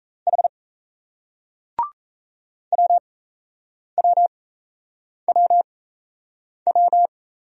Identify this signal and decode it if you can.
Morse code, heard as a keyed tone, about 38 wpm, spaced as for 8 wpm, 700 Hz. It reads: H WWWW